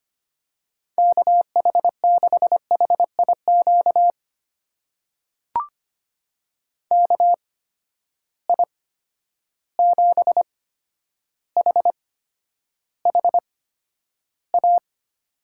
Morse code, heard as a keyed tone, about 25 wpm, spaced as for 11 wpm, 700 Hz. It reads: KH6HIQ KI7HHA
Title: USA Call Signs - 6 Characters - 8x Character Spacing - 25wpm